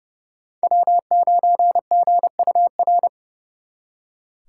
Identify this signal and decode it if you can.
Morse code, heard as a keyed tone, about 37 wpm, 700 Hz. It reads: W9GUR
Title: Speed Racing - USA 1x3 Call Signs 30wpm